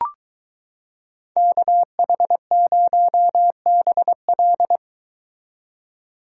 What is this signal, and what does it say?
Morse code, heard as a keyed tone, about 23 wpm, 700 Hz. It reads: KH0BL